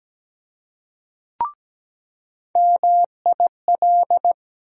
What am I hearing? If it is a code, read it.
Morse code, heard as a keyed tone, about 17 wpm, 700 Hz. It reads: MIL
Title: USA Call Sign Postfixes - 3 Letters 17wpm